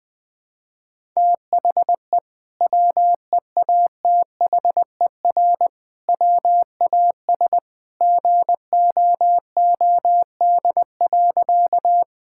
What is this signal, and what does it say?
Morse code, heard as a keyed tone, about 20 wpm, 700 Hz. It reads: THE WEATHER WAS GOOD.
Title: Sentences from Top 1000 Words - Review Part 5 - 20wpm